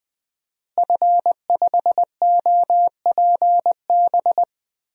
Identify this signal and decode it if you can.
Morse code, heard as a keyed tone, about 20 wpm, 700 Hz. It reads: F5OPB
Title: DXCC-100 Call Signs - 5 Characters - Repeated 2x - 20wpm